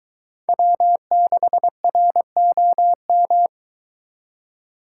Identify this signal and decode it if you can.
Morse code, heard as a keyed tone, about 23 wpm, 700 Hz. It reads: W6ROM